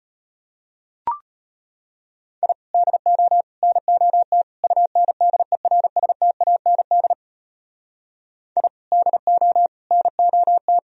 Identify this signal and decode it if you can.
Morse code, heard as a keyed tone, about 33 wpm, 700 Hz. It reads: I DO NOT UNDERSTAND I DO NOT